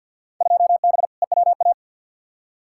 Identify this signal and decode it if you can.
Morse code, heard as a keyed tone, about 51 wpm, 700 Hz. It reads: 2B EWA
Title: Field Day Exchanges Encore - 50wpm